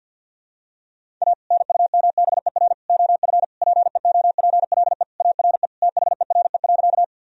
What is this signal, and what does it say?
Morse code, heard as a keyed tone, about 47 wpm, 700 Hz. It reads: A NUMBER OF PEOPLE ARE THERE.
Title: Sentences from Top 1000 Words - Review Part 1 - 50wpm